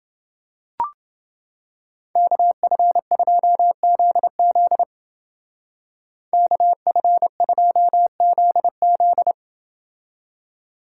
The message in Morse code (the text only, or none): KF2ZZ KF2ZZ